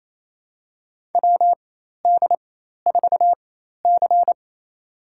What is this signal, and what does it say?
Morse code, heard as a keyed tone, about 28 wpm, spaced as for 13 wpm, 700 Hz. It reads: WD4C
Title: USA Call Signs - 4 Characters - 4x Character Spacing - 28wpm